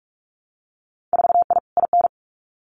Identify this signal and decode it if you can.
Morse code, heard as a keyed tone, about 44 wpm, 700 Hz. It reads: 4I IN
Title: Winter Field Day Exchanges Encore 45wpm